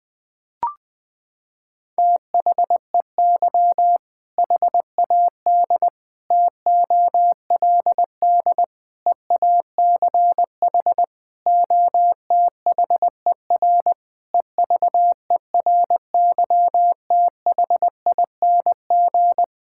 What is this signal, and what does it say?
Morse code, heard as a keyed tone, about 20 wpm, 700 Hz. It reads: THEY HAD TOLD EACH OTHER EVERYTHING
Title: Sentences from Top 1600 Words 20wpm